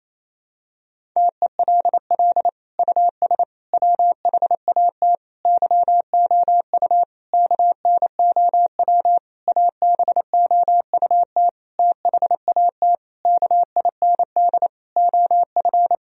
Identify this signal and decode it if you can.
Morse code, heard as a keyed tone, about 28 wpm, 700 Hz. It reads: TELL US WHAT YOU KNOW ABOUT THAT KIND OF
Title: Sentences from Top 1000 Words - Review Part 4 - 28wpm